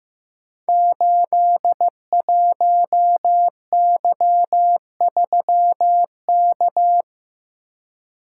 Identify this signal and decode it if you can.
Morse code, heard as a keyed tone, about 15 wpm, 700 Hz. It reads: 81Y3K
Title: Code Groups - 5 Letter:Numbers - 15wpm